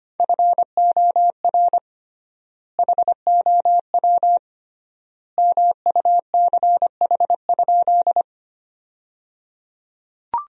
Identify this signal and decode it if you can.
Morse code, heard as a keyed tone, about 25 wpm, 700 Hz. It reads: FOR HOW MUCH?
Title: 3x Word-Spacing - Sentences from Top 200 Words - 25wpm